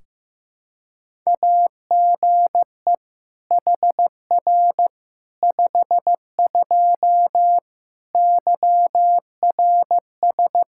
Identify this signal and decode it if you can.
Morse code, heard as a keyed tone, about 16 wpm, 700 Hz. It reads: AGE HR 52 YRS
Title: Practice QSO - Exchange 3 - More Info - 15wpm